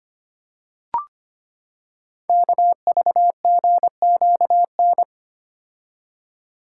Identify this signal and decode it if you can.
Morse code, heard as a keyed tone, about 25 wpm, 700 Hz. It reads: KVGQN
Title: Code Groups - 5 Letters - 25wpm